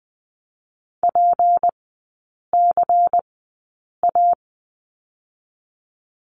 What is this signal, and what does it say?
Morse code, heard as a keyed tone, about 20 wpm, 700 Hz. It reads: P C A
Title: 198 Lesson 32e - Letter k - Three Characters 20wpm